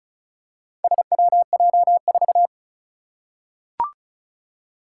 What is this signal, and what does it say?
Morse code, heard as a keyed tone, about 36 wpm, 700 Hz. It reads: SWJ4
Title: Four Challenging Letter-Number Combinations 35wpm